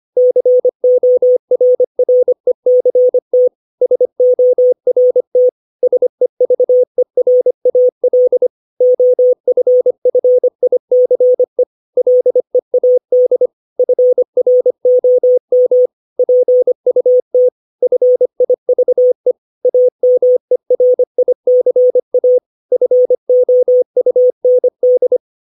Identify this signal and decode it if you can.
Morse code, heard as a keyed tone, about 25 wpm, 500 Hz. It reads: CORRECT SORT SEVERAL OFFICE LEAD FROM PUT FIVE AMERICA FOUND
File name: 632 words / 632 Words 25WPM